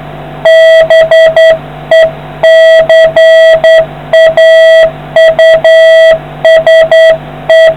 Morse code, heard as a keyed tone, about 10 wpm, 650 Hz, with noise. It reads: BECAUSE